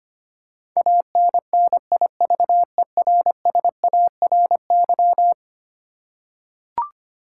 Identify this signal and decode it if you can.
Morse code, heard as a keyed tone, about 25 wpm, 700 Hz. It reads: ANNIVERSARY
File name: practice / 25wpm / Eleven Letter Words - 25wpm